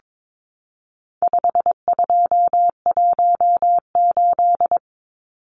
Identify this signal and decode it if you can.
Morse code, heard as a keyed tone, about 22 wpm, 700 Hz. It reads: 5218
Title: Numbers - 4-Digit 22wpm